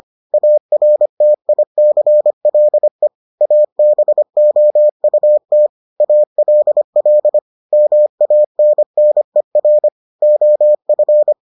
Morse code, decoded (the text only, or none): ARTICLE ABOUT ALL MANNER OF